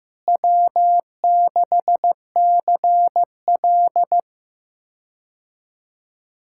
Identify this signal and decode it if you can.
Morse code, heard as a keyed tone, about 15 wpm, 700 Hz. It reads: W6CL
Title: K1USN SST QSO Practice - 15wpm